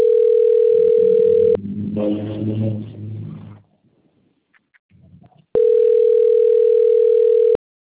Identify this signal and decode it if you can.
Ringback tone (North America), 440 + 480 Hz.
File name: out- - - - - .38099